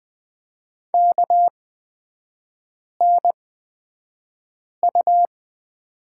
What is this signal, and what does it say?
Morse code, heard as a keyed tone, about 20 wpm, spaced as for 5 wpm, 700 Hz. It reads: KNU
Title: 224 Lesson 36e - BT - Three Characters 20wpm